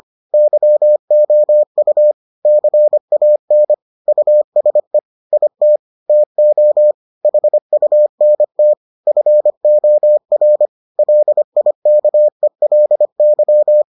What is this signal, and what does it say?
Morse code, heard as a keyed tone, about 25 wpm, 600 Hz. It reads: YOU CAN USE IT TO HUNT FOR LIKELY